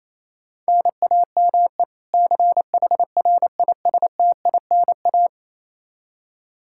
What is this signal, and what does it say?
Morse code, heard as a keyed tone, about 27 wpm, 700 Hz. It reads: NAME CHRISTINA
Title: Practice QSO - Exchange 1 - Beginning - 28wpm